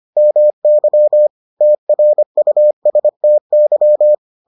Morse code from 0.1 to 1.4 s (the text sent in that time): MY